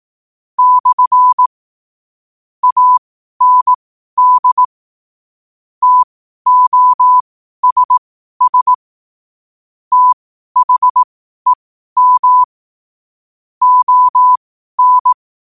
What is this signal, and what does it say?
Morse code, heard as a keyed tone, about 18 wpm, spaced as for 12 wpm, 1000 Hz. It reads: / AND TOSS THEM ON
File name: qotdfiles / 12wpm